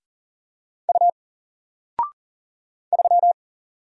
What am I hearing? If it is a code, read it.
Morse code, heard as a keyed tone, about 40 wpm, 700 Hz. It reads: U 3
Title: Single Letter-Number Alt 40wpm